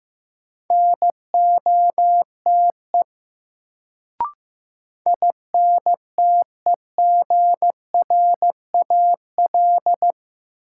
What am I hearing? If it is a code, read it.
Morse code, heard as a keyed tone, about 15 wpm, 700 Hz. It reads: NOTE INTEGRAL
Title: Sets of 5 Words - New Word Drill - 15wpm